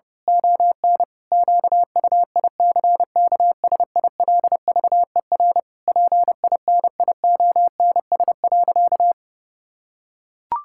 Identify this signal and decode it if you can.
Morse code, heard as a keyed tone, about 30 wpm, 700 Hz. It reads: ON QUICKSILVER PINIONS.
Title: Pangram Challenge v2 - 30wpm